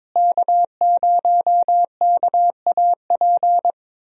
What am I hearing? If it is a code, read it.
Morse code, heard as a keyed tone, about 22 wpm, 700 Hz. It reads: K0KAP
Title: USA Call Signs - 5 Characters - Repeated 3x - 22wpm